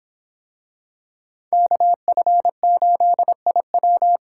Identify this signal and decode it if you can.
Morse code, heard as a keyed tone, about 25 wpm, 700 Hz. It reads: KF8IW